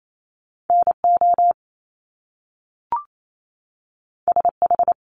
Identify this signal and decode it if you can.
Morse code, heard as a keyed tone, about 28 wpm, 700 Hz. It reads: NO SH
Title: Top Two Letter Combinations in Words 28wpm